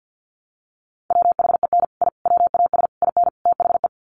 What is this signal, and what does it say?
Morse code, heard as a keyed tone, about 48 wpm, 700 Hz. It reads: WHEN I WAS IN THE
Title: Sets of 5 Words Encore2 50wpm